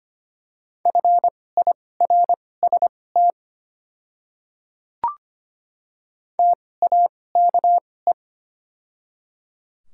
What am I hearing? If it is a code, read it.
Morse code, heard as a keyed tone, about 25 wpm, spaced as for 18 wpm, 700 Hz. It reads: FIRST TAKE
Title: 2x Character Spacing - Top 300 Words - 25wpm